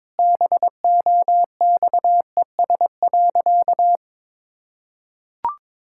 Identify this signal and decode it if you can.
Morse code, heard as a keyed tone, about 22 wpm, 700 Hz. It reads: BOXES.
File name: practice / 22wpm / Sentences with 3-Word Phrases - 22wpm